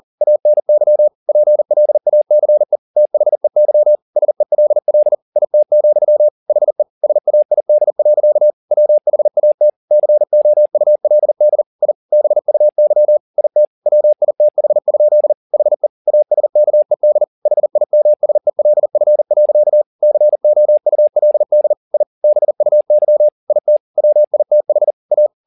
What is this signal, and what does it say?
Morse code, heard as a keyed tone, about 40 wpm, 600 Hz. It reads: ANY PLACE THEY SELL IT, HE SAID. WHAT COULD I BUY IT WITH? HE ASKED HIMSELF. COULD I BUY IT WITH A
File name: the old man and the sea 2 40wpm